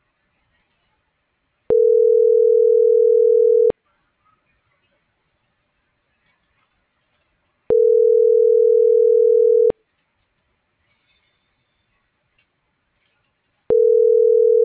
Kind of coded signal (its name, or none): telephone call progress tone